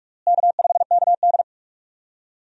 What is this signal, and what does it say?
Morse code, heard as a keyed tone, about 46 wpm, 700 Hz. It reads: K5KD
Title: USA Call Signs - 4 Characters - Repeated 2x - 45wpm